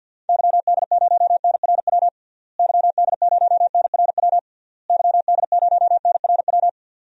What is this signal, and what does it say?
Morse code, heard as a keyed tone, about 52 wpm, 700 Hz. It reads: YD0NRW YD0NRW YD0NRW
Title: DXCC-100 Call Signs - 6 Characters - Repeated 3x - 50wpm